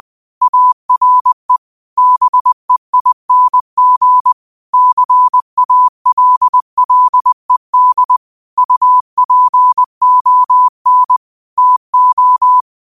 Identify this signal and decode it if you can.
Morse code, heard as a keyed tone, about 20 wpm, 1000 Hz. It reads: ARE BEING CALLED UPON TO
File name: qotdfiles / 20wpm